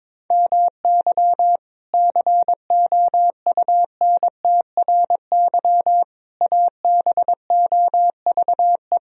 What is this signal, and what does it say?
Morse code, heard as a keyed tone, about 22 wpm, 700 Hz. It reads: MY COUNTRY ABOVE
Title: Sentences from Top 1000 Words - Review Part 2 - 22wpm